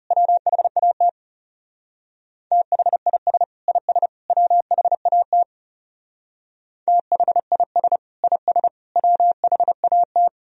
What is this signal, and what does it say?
Morse code, heard as a keyed tone, about 39 wpm, 700 Hz. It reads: WHAT THIS IS WHAT THIS IS WHAT